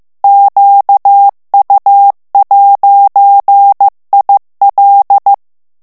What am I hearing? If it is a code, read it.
Morse code, heard as a keyed tone, about 15 wpm, 800 Hz. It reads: QU'IL
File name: morse mp3 / FRANCAIS TEXTE CLAIR 2 VITESSE 14 MOTS MINUTES